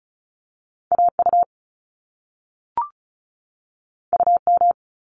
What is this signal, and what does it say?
Morse code, heard as a keyed tone, about 36 wpm, 700 Hz. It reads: AU UM
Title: Top Two Letter Combinations in Words 35wpm